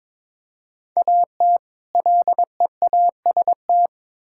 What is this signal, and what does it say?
Morse code, heard as a keyed tone, about 22 wpm, 700 Hz. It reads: AT LEAST
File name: Build A Phrase - 2-Word Phrases - 22wpm